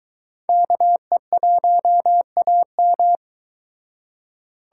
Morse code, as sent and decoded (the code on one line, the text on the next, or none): -.- . .---- .- --
KE1AM